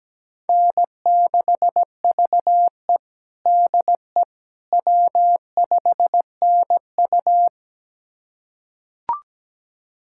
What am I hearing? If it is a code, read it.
Morse code, heard as a keyed tone, about 17 wpm, 700 Hz. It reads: N6VE DE W5NU
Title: USA Callsign DE Callsign - 4 Characters - 17wpm